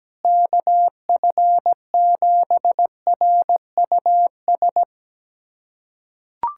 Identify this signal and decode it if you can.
Morse code, heard as a keyed tone, about 17 wpm, 700 Hz. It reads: KF7RUS
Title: USA Callsign DE Callsign - 6 Characters - 17wpm